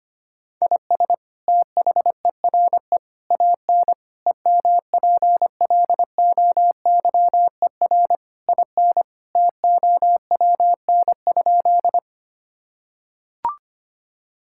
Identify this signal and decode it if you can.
Morse code, heard as a keyed tone, about 25 wpm, 700 Hz. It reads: IS THERE AN EMPLOYER IN TOWN?
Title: Sentences from Top 1600 Words 25wpm